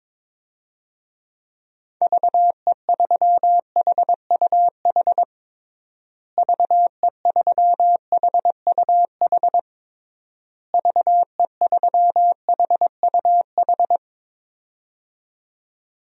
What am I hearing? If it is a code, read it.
Morse code, heard as a keyed tone, about 21 wpm, 700 Hz. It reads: VE3HUH VE3HUH VE3HUH
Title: DXCC-100 Call Signs - 6 Characters - Repeated 3x - 22wpm